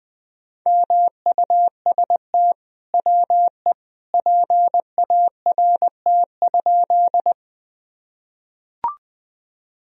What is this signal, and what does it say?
Morse code, heard as a keyed tone, about 20 wpm, 700 Hz. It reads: MUST WE PART?
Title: Sentences from Top 1000 Words - Review Part 1 - 20wpm